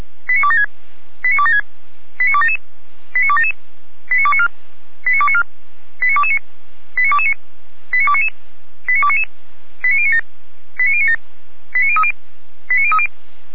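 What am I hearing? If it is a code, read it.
Five-tone selective call (ZVEI), heard as quick runs of five tones, tones of about 70 ms. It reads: 89276 89276 89277 89277 89294 89294 89229 89229 89299 89299 89097 89097 89030 89030
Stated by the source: ZVEI five-tone selective call received with an SDR.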